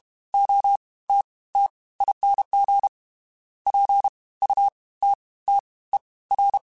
Morse code, heard as a keyed tone, about 32 wpm, 800 Hz. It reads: O T T ING P U T T E R
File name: Special INGs / Special INGs-mp3s / Fast ing @30